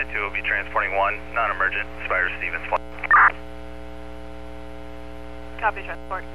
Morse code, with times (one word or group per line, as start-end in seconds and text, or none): none